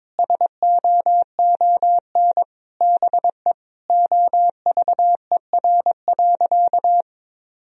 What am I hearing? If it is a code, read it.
Morse code, heard as a keyed tone, about 22 wpm, 700 Hz. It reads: SOON BE OVER.
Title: Sentences from Top 1000 Words - Review Part 2 - 22wpm